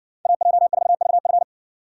Morse code, heard as a keyed tone, about 60 wpm, 700 Hz. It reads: AJ4VF